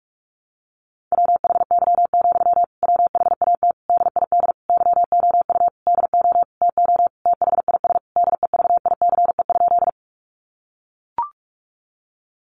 Morse code, heard as a keyed tone, about 44 wpm, 700 Hz. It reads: WHY, WHAT DID YOU DO TO THIS DEVICE?
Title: Sentences from Top 1500 Words 45wpm